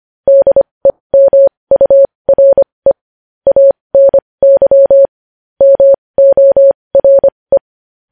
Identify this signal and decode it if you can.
Morse code, heard as a keyed tone, about 25 wpm, spaced as for 20 wpm, 550 Hz. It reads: DEMURE ANY MORE
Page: news-headlines-in-morse-code-at-20-wpm